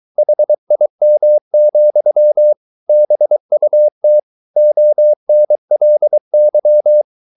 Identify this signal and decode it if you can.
Morse code, heard as a keyed tone, about 23 wpm, 600 Hz. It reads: HIM, BUT ONLY